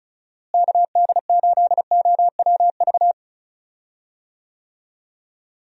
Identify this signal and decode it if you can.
Morse code, heard as a keyed tone, about 36 wpm, 700 Hz. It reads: KD8OWV
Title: USA Call Signs - 6 Characters - Repeated 2x - 35wpm